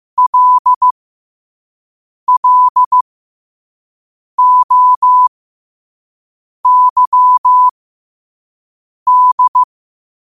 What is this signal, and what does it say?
Morse code, heard as a keyed tone, about 15 wpm, spaced as for 5 wpm, 1000 Hz. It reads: LLOYD
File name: qotdfiles / 5wpm